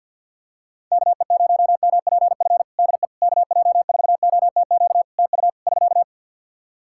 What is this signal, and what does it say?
Morse code, heard as a keyed tone, about 53 wpm, 700 Hz. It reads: KE0MPF DE KJ4OTQ TU <SK>
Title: Practice QSO - Exchange 4 - Ending - 50wpm